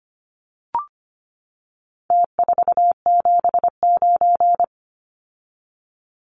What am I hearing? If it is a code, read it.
Morse code, heard as a keyed tone, about 25 wpm, 700 Hz. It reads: T479
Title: Numbers - Contest Style - 4-Digit 25wpm